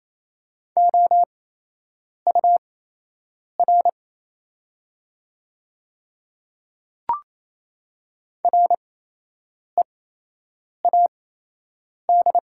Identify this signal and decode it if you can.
Morse code, heard as a keyed tone, about 28 wpm, spaced as for 8 wpm, 690 Hz. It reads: OUR READ
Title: 8x Character Spacing - Top 300 Words - 28wpm